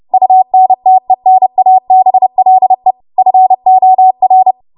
Morse code, heard as a keyed tone, about 30 wpm, 750 Hz. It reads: UNTENABLE FOR
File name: Morse / 30WPM